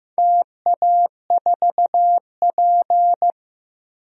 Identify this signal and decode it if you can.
Morse code, heard as a keyed tone, about 15 wpm, 700 Hz. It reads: TA4P
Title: DXCC-100 Call Signs - 4 Characters - Repeated 2x - 15wpm